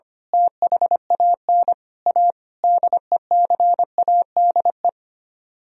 Morse code, heard as a keyed tone, about 24 wpm, 700 Hz. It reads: THAN A DECADE
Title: Speed-Racing - Sets of 5 Words 25wpm